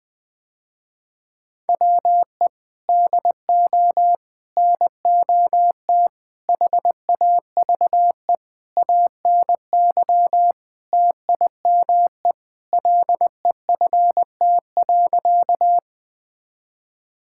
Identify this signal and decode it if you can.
Morse code, heard as a keyed tone, about 20 wpm, 700 Hz. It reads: WE DO NOT HAVE ANY TIME LEFT.
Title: Sentences with 4-Word Phrases - Spoken Beginning - 20wpm